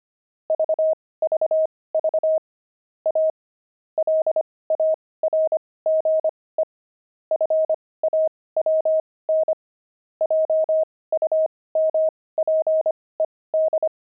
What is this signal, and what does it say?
Morse code, heard as a keyed tone, about 25 wpm, spaced as for 18 wpm, 630 Hz. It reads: VVV A LARGE FAWN JUMPED
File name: pangram-fs-18-cs-25-zinc